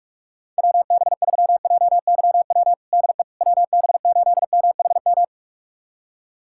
Morse code, heard as a keyed tone, about 47 wpm, 700 Hz. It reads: WB3JYW DE WB8MHM
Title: USA Callsign DE Callsign - 6 Characters - 45wpm